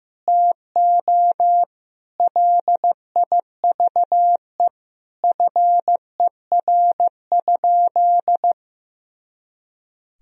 Morse code, heard as a keyed tone, about 15 wpm, 700 Hz. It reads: TO LIVE FER?
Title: Sentences from Top 200 Words and Abbreviations 15wpm